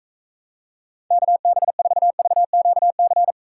Morse code, heard as a keyed tone, about 40 wpm, 700 Hz. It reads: KB4VQC